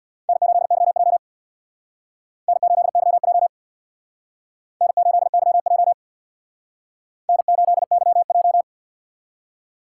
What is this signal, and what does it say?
Morse code, heard as a keyed tone, about 59 wpm, 700 Hz. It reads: N8YJ N8YJ N8YJ N8YJ
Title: Speed Racing - USA 1x2 Call Signs 50wpm